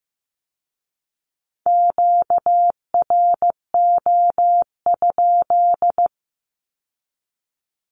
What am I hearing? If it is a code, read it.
Morse code, heard as a keyed tone, about 15 wpm, 700 Hz. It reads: QRO?
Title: Q-Codes with Spoken Meaning - 15wpm